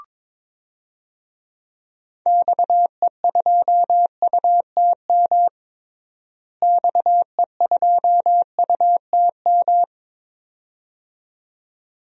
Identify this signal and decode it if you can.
Morse code, heard as a keyed tone, about 22 wpm, 700 Hz. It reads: XE2UTM XE2UTM
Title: DXCC-100 Call Signs - 6 Characters - Repeated 2x - 22wpm